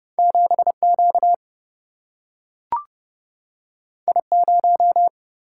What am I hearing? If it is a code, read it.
Morse code, heard as a keyed tone, about 29 wpm, 700 Hz. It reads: 7 Q I 0
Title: Single-Letter Single-Number 30wpm